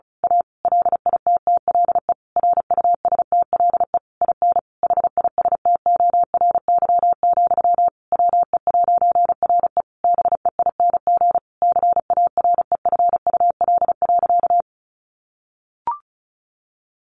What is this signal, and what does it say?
Morse code, heard as a keyed tone, about 34 wpm, 700 Hz. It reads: A LITTLE RUSTLE IN HISTORY, WE'RE BEING CAREFUL.
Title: A Sound Of Thunder 35wpm